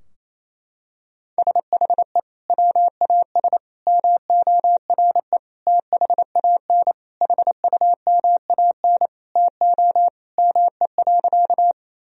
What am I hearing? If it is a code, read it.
Morse code, heard as a keyed tone, about 28 wpm, 700 Hz. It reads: SHE WAS MORE THAN HUMAN TO ME.
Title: Sentences from Top 1000 Words - Review Part 3 - 28wpm